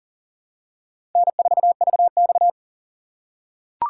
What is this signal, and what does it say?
Morse code, heard as a keyed tone, about 40 wpm, 700 Hz. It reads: N4VX
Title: USA Call Signs - 4 Characters - 3x Character Spacing - 40wpm